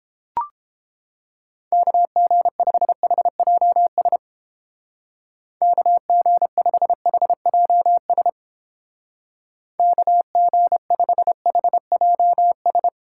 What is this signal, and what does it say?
Morse code, heard as a keyed tone, about 29 wpm, 700 Hz. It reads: KG5HJS KG5HJS KG5HJS